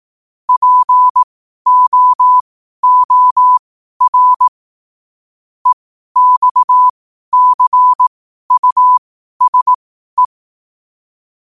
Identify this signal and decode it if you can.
Morse code, heard as a keyed tone, about 18 wpm, spaced as for 12 wpm, 1000 Hz. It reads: POOR EXCUSE
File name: qotdfiles / 12wpm